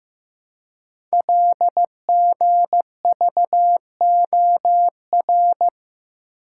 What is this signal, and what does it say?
Morse code, heard as a keyed tone, about 15 wpm, 700 Hz. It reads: LGVOR
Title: Code Groups - 5 Letters - 15wpm